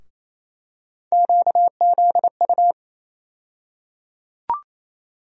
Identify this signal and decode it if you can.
Morse code, heard as a keyed tone, about 28 wpm, 700 Hz. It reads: QZU
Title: USA Call Sign Postfixes - 3 Letters 28wpm